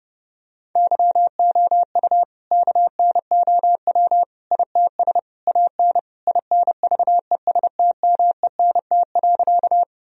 Morse code, heard as a keyed tone, about 32 wpm, 700 Hz. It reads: YOU KNOW ITS AN INVESTMENT.
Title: Sentences from Top 1000 Words - Review Part 4 - 30wpm